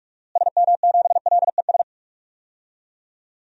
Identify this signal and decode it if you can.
Morse code, heard as a keyed tone, about 46 wpm, 700 Hz. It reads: SM7LES
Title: DXCC-100 Call Signs - 6 Characters - Repeated 2x - 45wpm